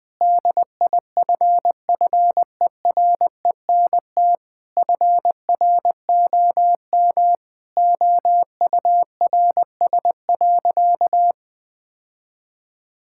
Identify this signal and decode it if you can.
Morse code, heard as a keyed tone, about 20 wpm, 700 Hz. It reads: DIFFERENT FROM OURS.
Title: Sentences from Top 1000 Words - Review Part 5 - 20wpm